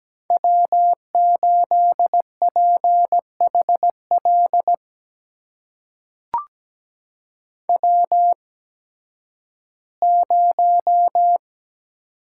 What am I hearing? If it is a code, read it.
Morse code, heard as a keyed tone, about 17 wpm, spaced as for 11 wpm, 700 Hz. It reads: W8PHL W0
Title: USA Call Signs - 8x Character Spacing - 17wpm